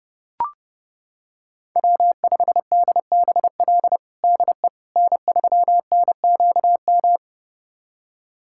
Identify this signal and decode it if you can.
Morse code, heard as a keyed tone, about 32 wpm, 700 Hz. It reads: W5DBL DE N3NQM
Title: USA Callsign DE Callsign - 5 Characters - 30wpm